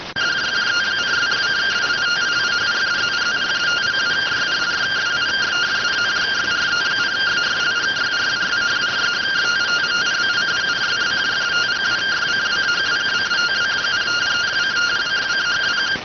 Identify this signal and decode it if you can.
Two-tone teleprinter signal, about 170 Hz shift.